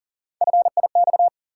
Morse code, heard as a keyed tone, about 40 wpm, 700 Hz. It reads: FIX
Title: Three Letter Words - 40wpm